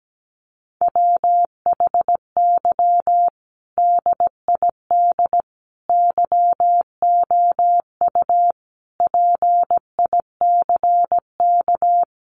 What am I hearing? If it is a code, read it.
Morse code, heard as a keyed tone, about 17 wpm, 700 Hz. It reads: WHY DID YOU PICK
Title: Sentences from Top 1000 Words 17wpm